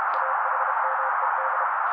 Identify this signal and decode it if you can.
Morse code, heard as a keyed tone, about 32 wpm, 530 Hz. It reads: AHMED
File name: daily / 31wpm noise